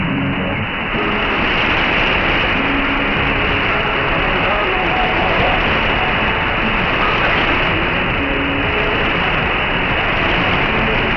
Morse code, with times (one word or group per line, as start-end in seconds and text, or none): none